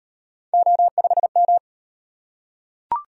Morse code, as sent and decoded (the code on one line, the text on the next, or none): --- ..... --
O5M